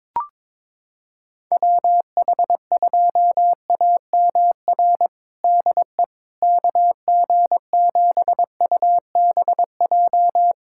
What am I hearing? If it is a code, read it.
Morse code, heard as a keyed tone, about 22 wpm, 700 Hz. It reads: WH2AMR DE KG7UBJ
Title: USA Callsign DE Callsign - 6 Characters - 22wpm